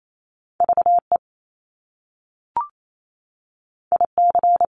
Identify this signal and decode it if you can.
Morse code, heard as a keyed tone, about 28 wpm, 700 Hz. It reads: VE IC
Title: Top Two Letter Combinations in Words 28wpm